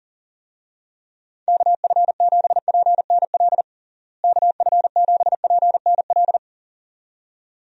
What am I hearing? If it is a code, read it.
Morse code, heard as a keyed tone, about 40 wpm, 700 Hz. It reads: KF7PNL KF7PNL
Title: USA Call Signs - 6 Characters - Repeated 2x - 40wpm